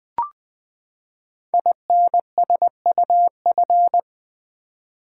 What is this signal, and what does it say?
Morse code, heard as a keyed tone, about 20 wpm, 700 Hz. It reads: INSUF
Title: Build A Word - 5 Letters At A Time - 12-Letter Words - 20wpm